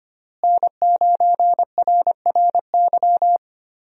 Morse code, heard as a keyed tone, about 25 wpm, 700 Hz. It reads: N9RRY
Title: USA Call Signs - 5 Characters - Repeated 2x - 25wpm